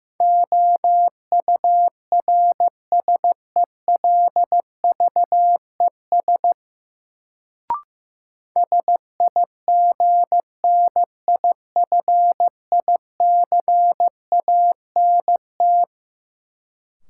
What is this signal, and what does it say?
Morse code, heard as a keyed tone, about 15 wpm, 700 Hz. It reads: OURSELVES SIGNIFICANT
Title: Sentences from Top 1600 Words - New Word Drill Alt - 15wpm